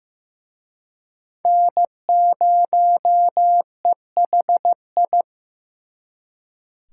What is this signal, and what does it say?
Morse code, heard as a keyed tone, about 18 wpm, 700 Hz. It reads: N0EHI
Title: Speed Racing - USA 1x3 Call Signs 15wpm